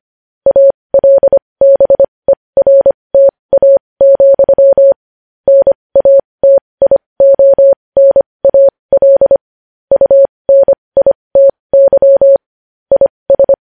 Morse code, heard as a keyed tone, about 25 wpm, spaced as for 20 wpm, 550 Hz. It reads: ALBERTA, NATIONAL UNITY IS
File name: podcast / News headlines-1x-25x20